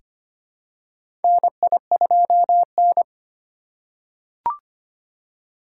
Morse code, heard as a keyed tone, about 25 wpm, 700 Hz. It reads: NI2N
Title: USA Call Signs - 4 Characters - 2x Character Spacing - 25wpm